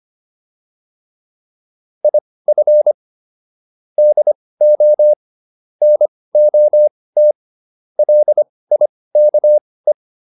Morse code, heard as a keyed tone, about 25 wpm, spaced as for 18 wpm, 600 Hz. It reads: IF DO NOT LIKE
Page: intermediate-practice-files